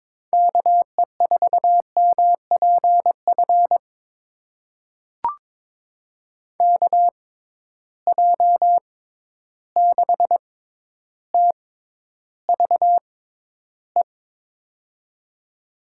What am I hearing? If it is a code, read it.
Morse code, heard as a keyed tone, about 22 wpm, spaced as for 11 wpm, 700 Hz. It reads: KE4MPF KJ6TVE
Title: USA Call Signs - 6 Characters - 6x Character Spacing - 22wpm